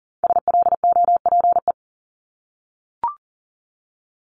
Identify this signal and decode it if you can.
Morse code, heard as a keyed tone, about 40 wpm, 700 Hz. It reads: SLOPE
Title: Multi-letter Phonemes Repeated 3x 40wpm